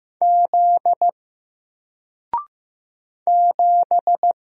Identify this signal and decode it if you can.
Morse code, heard as a keyed tone, about 15 wpm, 700 Hz. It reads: Z 7
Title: Single Letter-Number 15wpm